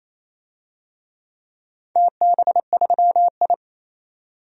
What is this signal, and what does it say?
Morse code, heard as a keyed tone, about 28 wpm, 700 Hz. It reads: TB3I
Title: DXCC-100 Call Signs - 4 Characters - 28wpm